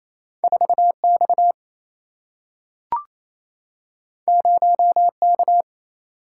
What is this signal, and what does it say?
Morse code, heard as a keyed tone, about 28 wpm, 700 Hz. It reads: 4 X 0 K
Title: Single-Letter Single-Number 28wpm